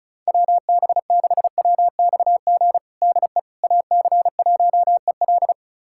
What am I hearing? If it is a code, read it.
Morse code, heard as a keyed tone, about 34 wpm, 700 Hz. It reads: WB6WXG DE AC1EL
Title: USA Callsign DE Callsign 35wpm